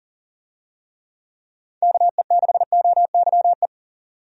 Morse code, heard as a keyed tone, about 40 wpm, 700 Hz. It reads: KE6OYE
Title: USA Call Signs - 6 Characters - 40wpm